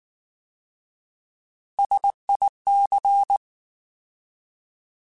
Morse code, heard as a keyed tone, about 19 wpm, 800 Hz. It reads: SIC